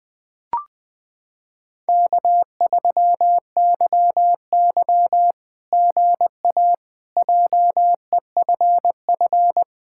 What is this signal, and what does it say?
Morse code, heard as a keyed tone, about 20 wpm, 700 Hz. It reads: K3YY GA JEFF
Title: SOTA QSO Practice 20wpm